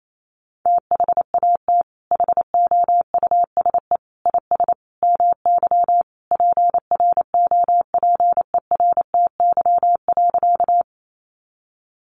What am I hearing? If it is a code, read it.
Morse code, heard as a keyed tone, about 28 wpm, 700 Hz. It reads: THAT HOUSE IS MY PROPERTY.
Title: Sentences from Top 1000 Words 28wpm